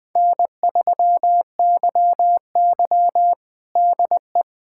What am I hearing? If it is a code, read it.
Morse code, heard as a keyed tone, about 20 wpm, 700 Hz. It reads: N3YY DE
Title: USA Callsign DE Callsign - 4 Characters - 20wpm